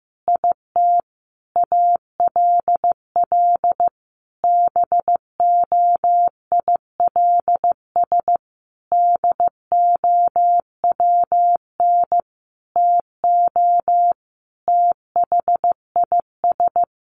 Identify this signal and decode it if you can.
Morse code, heard as a keyed tone, about 15 wpm, 700 Hz. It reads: IT ALL BOILS DOWN TO THIS
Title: A Sound Of Thunder 15wpm